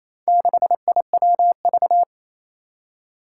Morse code, heard as a keyed tone, about 29 wpm, 700 Hz. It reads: 6IWV
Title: Four Challenging Letter-Number Combinations 28wpm